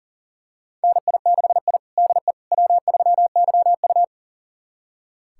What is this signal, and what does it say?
Morse code, heard as a keyed tone, about 40 wpm, 700 Hz. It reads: NI6I DE W3YU
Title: USA Callsign DE Callsign - 4 Characters - 40wpm